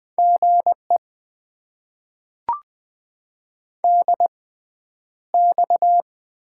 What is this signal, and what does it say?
Morse code, heard as a keyed tone, about 20 wpm, spaced as for 10 wpm, 700 Hz. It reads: GE DX
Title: Top 90 QSO Elements - 6x Character Spacing - 20wpm